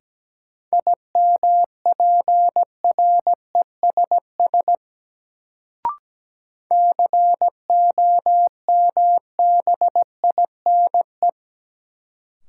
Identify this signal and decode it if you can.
Morse code, heard as a keyed tone, about 17 wpm, 700 Hz. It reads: IMPRESS COMBINE
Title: Sentences from Top 1600 Words - New Word Drill - 17wpm